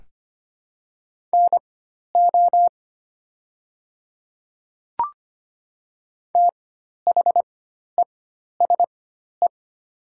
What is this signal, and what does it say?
Morse code, heard as a keyed tone, about 25 wpm, spaced as for 12 wpm, 700 Hz. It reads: NO THESE
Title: 4x Character Spacing - Top 100 Words - 25wpm